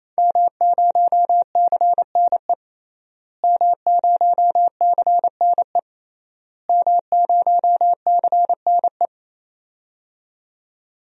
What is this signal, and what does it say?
Morse code, heard as a keyed tone, about 28 wpm, 700 Hz. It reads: M0CNE M0CNE M0CNE
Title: DXCC-100 Call Signs - 5 Characters - Repeated 3x - 28wpm